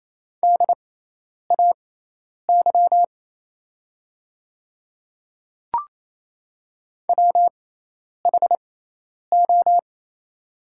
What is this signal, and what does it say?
Morse code, heard as a keyed tone, about 28 wpm, spaced as for 10 wpm, 700 Hz. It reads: DAY WHO
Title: 6x Character Spacing - Top 200 Words - 28wpm